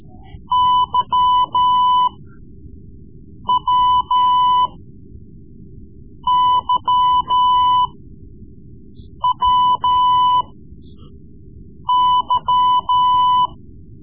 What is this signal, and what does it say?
Morse code, heard as a keyed tone, about 14 wpm, spaced as for 5 wpm, 1000 Hz. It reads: YWYWY
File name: Morse Lessons / Morse Lessons / Tape-3B